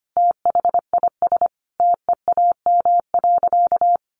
Morse code, heard as a keyed tone, about 25 wpm, 700 Hz. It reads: THIS TEAM.
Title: Sentences from Top 1500 Words 25wpm